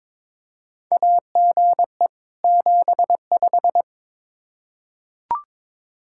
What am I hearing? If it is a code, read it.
Morse code, heard as a keyed tone, about 22 wpm, 700 Hz. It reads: AGE 75
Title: K1USN SST QSO Practice 22wpm